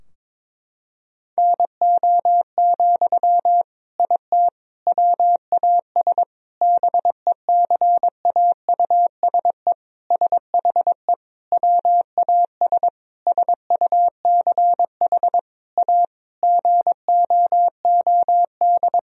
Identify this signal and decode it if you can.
Morse code, heard as a keyed tone, about 22 wpm, 700 Hz. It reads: NO, IT WAS BECAUSE SHE WAS SUCH A GOOD
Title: Sentences from Top 1000 Words - Review Part 4 - 22wpm